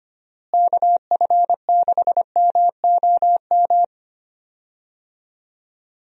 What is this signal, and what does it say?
Morse code, heard as a keyed tone, about 25 wpm, 700 Hz. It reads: KF6MOM
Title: USA Call Signs - 6 Characters - Repeated 3x - 25wpm